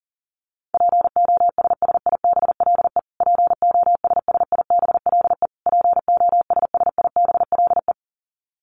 Morse code, heard as a keyed tone, about 40 wpm, 700 Hz. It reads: POSSIBLE POSSIBLE POSSIBLE
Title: Top 500 Words Repeated 40wpm